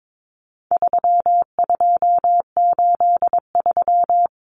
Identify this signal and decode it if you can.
Morse code, heard as a keyed tone, about 22 wpm, 700 Hz. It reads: 3283
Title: Common Cut Numbers - 4-Digit 22wpm